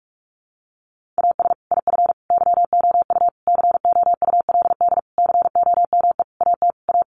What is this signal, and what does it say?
Morse code, heard as a keyed tone, about 44 wpm, 700 Hz. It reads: AS IF YOU COULD COME AT A
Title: Sentences from Top 900 Words 45wpm